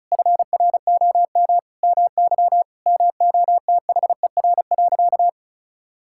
Morse code, heard as a keyed tone, about 34 wpm, 700 Hz. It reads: FROM MY MOTHER.
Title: Sentences from Top 1000 Words - Review Part 6 - 35wpm